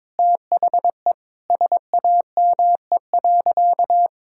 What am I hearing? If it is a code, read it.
Morse code, heard as a keyed tone, about 22 wpm, 700 Hz. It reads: THE SAME.
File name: Sentences with 9-Word Phrases - 22wpm